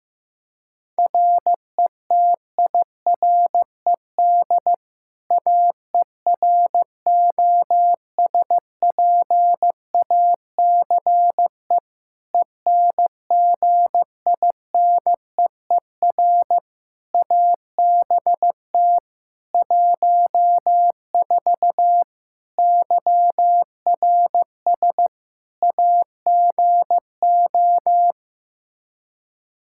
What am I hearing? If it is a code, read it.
Morse code, heard as a keyed tone, about 16 wpm, 700 Hz. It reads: RETIRED AEROSPACE ENGINEER ABT 14 YRS AGO
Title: K1USN SST QSO Practice 15wpm